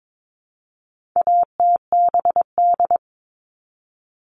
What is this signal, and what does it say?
Morse code, heard as a keyed tone, about 22 wpm, 700 Hz. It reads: ATBD
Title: All Cut Numbers - 4-Digit 22wpm